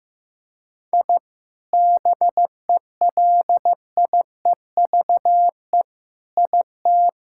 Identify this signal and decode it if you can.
Morse code, heard as a keyed tone, about 15 wpm, 700 Hz. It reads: I BELIEVE IT
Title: Sentences from Top 1000 Words - Review Part 5 - 15wpm